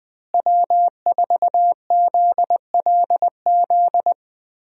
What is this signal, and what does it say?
Morse code, heard as a keyed tone, about 20 wpm, 700 Hz. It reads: W4ZLZ
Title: USA Call Signs - 5 Characters - Repeated 3x - 20wpm